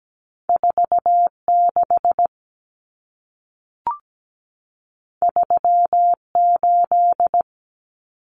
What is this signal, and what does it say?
Morse code, heard as a keyed tone, about 17 wpm, 700 Hz. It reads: 46 38
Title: Numbers - 2-Digit 17wpm